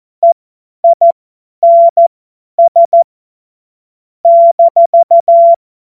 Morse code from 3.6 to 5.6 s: -